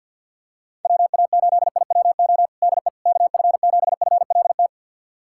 Morse code, heard as a keyed tone, about 46 wpm, 700 Hz. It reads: WA8IWO DE KF7FLT
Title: USA Callsign DE Callsign 50wpm